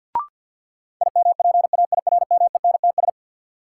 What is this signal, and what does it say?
Morse code, heard as a keyed tone, about 53 wpm, 700 Hz. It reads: IMPAIRMENTS
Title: Sets of 5 Words Encore - New Word Drill - 50wpm